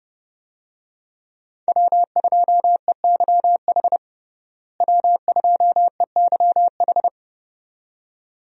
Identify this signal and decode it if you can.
Morse code, heard as a keyed tone, about 30 wpm, 700 Hz. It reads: W2EYH W2EYH
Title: USA Call Signs - 5 Characters - Repeated 2x - 30wpm